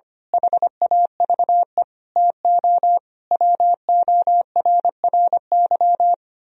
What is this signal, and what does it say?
Morse code, heard as a keyed tone, about 24 wpm, 700 Hz. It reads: HAVE TO WORRY
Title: Speed-Racing - Sets of 5 Words 25wpm